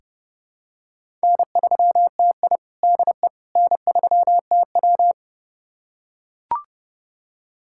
Morse code, heard as a keyed tone, about 32 wpm, 700 Hz. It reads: N3TI DE N3TW
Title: USA Callsign DE Callsign - 4 Characters - 30wpm